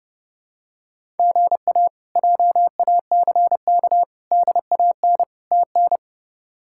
Morse code, heard as a keyed tone, about 31 wpm, 700 Hz. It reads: GA JACK DAN TN
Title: K1USN SST QSO Practice - 30wpm